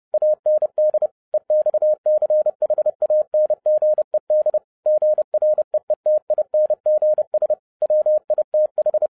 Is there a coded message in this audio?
Yes: Morse code from 0.1 to 9.1 s, heard as a keyed tone, about 30 wpm, 610 Hz. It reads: AND EXCHANGED GREETINGS WITH